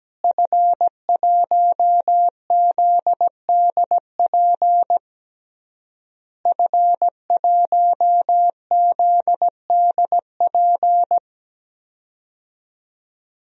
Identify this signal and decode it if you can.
Morse code, heard as a keyed tone, about 17 wpm, 700 Hz. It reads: F1ZDP F1ZDP
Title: DXCC-100 Call Signs - 5 Characters - Repeated 3x - 17wpm